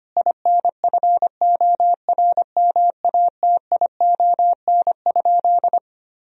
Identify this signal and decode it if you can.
Morse code, heard as a keyed tone, about 25 wpm, 700 Hz. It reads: INFORMATION?
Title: Sentences from Top 1000 Words - Review Part 3 - 25wpm